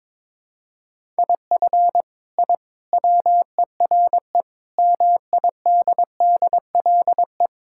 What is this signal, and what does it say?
Morse code, heard as a keyed tone, about 22 wpm, 700 Hz. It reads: IF I WERE MIDDLE
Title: Sentences from Top 1000 Words - Review Part 6 - 22wpm